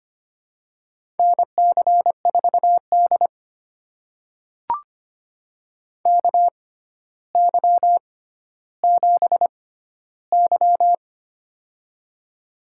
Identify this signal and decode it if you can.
Morse code, heard as a keyed tone, about 25 wpm, spaced as for 13 wpm, 700 Hz. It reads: NC4D KY7Y
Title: USA Call Signs - 4 Characters - 6x Character Spacing - 25wpm